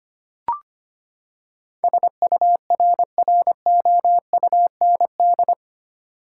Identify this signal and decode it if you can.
Morse code, heard as a keyed tone, about 25 wpm, 700 Hz. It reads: SURROUND
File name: Eight Letter Words - 25wpm